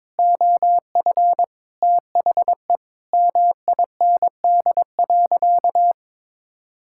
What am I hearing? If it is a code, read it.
Morse code, heard as a keyed tone, about 22 wpm, 700 Hz. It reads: OF THE MIND.
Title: Sentences from Top 1000 Words - Review Part 6 - 22wpm